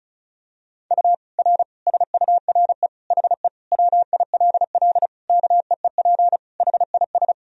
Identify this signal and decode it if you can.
Morse code, heard as a keyed tone, about 36 wpm, 700 Hz. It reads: U R SURE HE WILL KEEP HIS
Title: Sentences from Top 300 Words and Abbreviations 35wpm